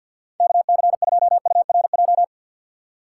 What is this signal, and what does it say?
Morse code, heard as a keyed tone, about 53 wpm, 700 Hz. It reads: KC2URJ
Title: USA Call Signs - 6 Characters - 4x Character Spacing - 50wpm